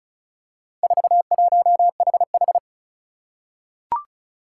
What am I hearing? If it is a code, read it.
Morse code, heard as a keyed tone, about 34 wpm, 700 Hz. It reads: 41HH
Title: Four Challenging Letter-Number Combinations 35wpm